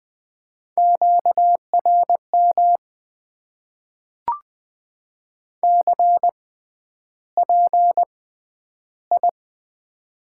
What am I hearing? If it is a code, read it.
Morse code, heard as a keyed tone, about 20 wpm, spaced as for 10 wpm, 700 Hz. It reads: QRM CPI
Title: Top 90 QSO Elements - 6x Character Spacing - 20wpm